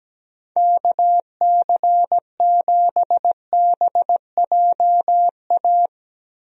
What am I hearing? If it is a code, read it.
Morse code, heard as a keyed tone, about 17 wpm, 700 Hz. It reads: KC7BJA
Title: USA Call Signs - 6 Characters - Repeated 3x - 17wpm